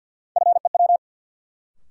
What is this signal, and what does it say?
Morse code, heard as a keyed tone, about 53 wpm, 700 Hz. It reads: FEW
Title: Sets of 2 Words - New Word Drill - 50wpm